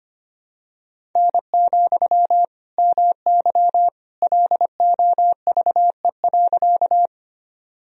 Morse code, heard as a keyed tone, about 25 wpm, 700 Hz. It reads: N, MY LOVE.
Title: Sentences from Top 400 Words and Abbreviations 25wpm